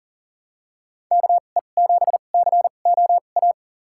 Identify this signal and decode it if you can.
Morse code, heard as a keyed tone, about 40 wpm, spaced as for 29 wpm, 700 Hz. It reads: KE7COA
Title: USA Call Signs - 6 Characters - 2x Character Spacing - 40wpm